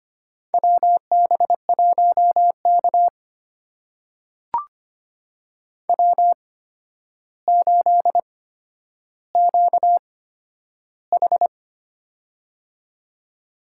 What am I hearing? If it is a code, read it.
Morse code, heard as a keyed tone, about 25 wpm, spaced as for 11 wpm, 700 Hz. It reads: WB1K W8QH
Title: USA Call Signs - 4 Characters - 8x Character Spacing - 25wpm